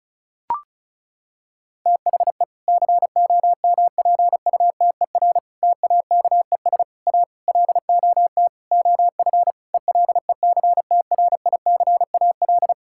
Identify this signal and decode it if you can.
Morse code, heard as a keyed tone, about 36 wpm, 700 Hz. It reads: THE COMPUTER TAKES A LOT OF ELECTRICAL
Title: Sentences from Top 1000 Words - Review Part 6 - 35wpm